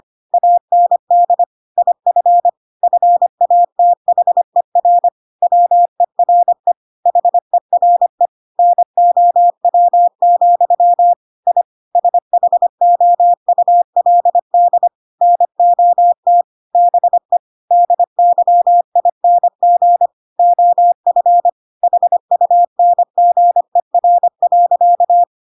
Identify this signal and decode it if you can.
Morse code, heard as a keyed tone, about 25 wpm, 700 Hz. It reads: AND IF FATHER WERE HERE NOW, I SHOULD NOT BE DYING OF HUNGER.
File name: Pinocchio / Pinocchio ch5 25 25 ik1hge